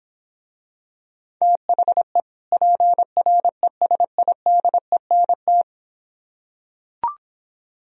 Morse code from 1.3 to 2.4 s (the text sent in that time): THE